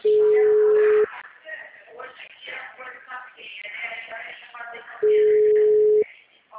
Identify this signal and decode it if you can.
Ringback tone (Europe, 425 Hz), with speech.
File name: out- -2048- - - .35916